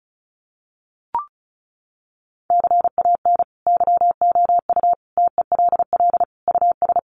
Morse code, heard as a keyed tone, about 34 wpm, 700 Hz. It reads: CAN YOU TELL US
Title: Sets of 5 Words Encore2 35wpm